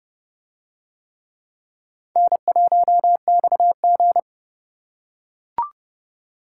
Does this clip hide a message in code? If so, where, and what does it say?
Yes: Morse code from 2.2 to 4.2 s, heard as a keyed tone, about 28 wpm, 700 Hz. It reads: N1XG